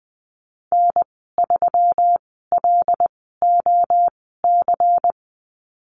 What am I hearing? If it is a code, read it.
Morse code, heard as a keyed tone, about 20 wpm, spaced as for 15 wpm, 700 Hz. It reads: N3LOC
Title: 140 Lesson 23g - Letter f - Call Signs 20wpm